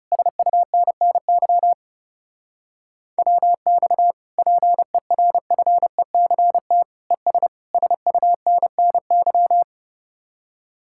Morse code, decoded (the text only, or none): SUNNY WX PERFECT ES SUNNY